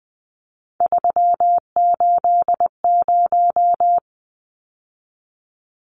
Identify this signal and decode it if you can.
Morse code, heard as a keyed tone, about 20 wpm, 700 Hz. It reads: 380
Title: Numbers - 3-Digit 20wpm